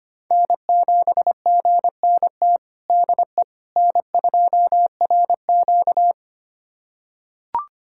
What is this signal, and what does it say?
Morse code, heard as a keyed tone, about 25 wpm, 700 Hz. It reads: N7GNT DE N2RQ
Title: USA Callsign DE Callsign 25wpm